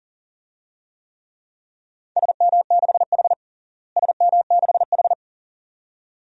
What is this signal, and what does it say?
Morse code, heard as a keyed tone, about 37 wpm, 700 Hz. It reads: SM6H SM6H
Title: DXCC-100 Call Signs - 4 Characters - Repeated 2x - 40wpm